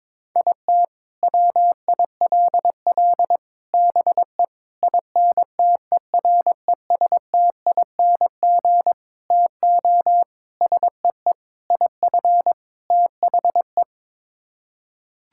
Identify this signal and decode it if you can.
Morse code, heard as a keyed tone, about 22 wpm, 700 Hz. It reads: IT WILL BE INTERESTING TO SEE IF THE
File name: practice / 22wpm / Sets of 8 Words - 22wpm